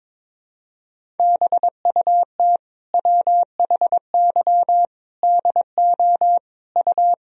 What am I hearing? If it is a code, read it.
Morse code, heard as a keyed tone, about 22 wpm, 700 Hz. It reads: BUT WHY DO U
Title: Sentences from Top 200 Words and Abbreviations 22wpm